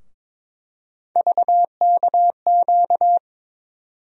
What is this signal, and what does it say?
Morse code, heard as a keyed tone, about 21 wpm, 700 Hz. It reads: VKQ
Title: USA Call Sign Postfixes - 3 Letters 22wpm